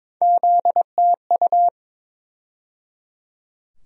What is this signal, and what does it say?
Morse code, heard as a keyed tone, about 22 wpm, 700 Hz. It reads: ZTU
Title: USA Call Sign Postfixes - 3 Letters 22wpm